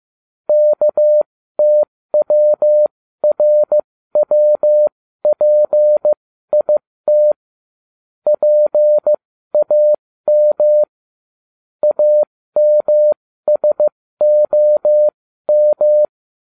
Morse code, heard as a keyed tone, about 15 wpm, spaced as for 13 wpm, 600 Hz. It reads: KTWRWPIT PAM AMSOM